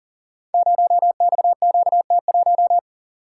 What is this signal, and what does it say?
Morse code, heard as a keyed tone, about 39 wpm, 700 Hz. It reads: 0XQT1
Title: Code Groups - 5 Letter:Numbers - 40wpm